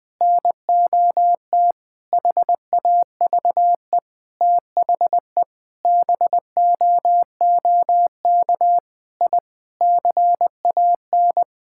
Morse code, heard as a keyed tone, about 20 wpm, 700 Hz. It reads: NOT HAVE THE BOOK I CAN
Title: Sentences with 5-Word Phrases - 20wpm